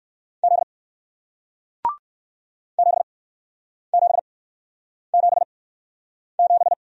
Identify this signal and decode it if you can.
Morse code, heard as a keyed tone, about 60 wpm, spaced as for 9 wpm, 700 Hz. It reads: Z 7777